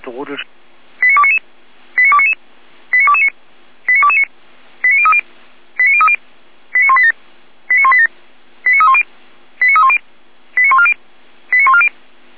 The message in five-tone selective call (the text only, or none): 89299 89299 89229 89229 89030 89030 89187 89187 89310 89310 89140 89140